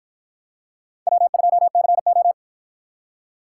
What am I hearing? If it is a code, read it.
Morse code, heard as a keyed tone, about 53 wpm, 700 Hz. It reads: W2CO